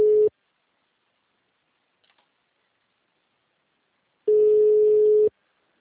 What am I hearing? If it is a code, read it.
Ringback tone (Europe, 425 Hz).
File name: out- -2051- - - .48459